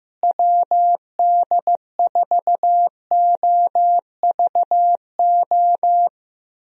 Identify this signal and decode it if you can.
Morse code, heard as a keyed tone, about 15 wpm, 700 Hz. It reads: WD4OVO
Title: USA Call Signs - 6 Characters - Repeated 2x - 15wpm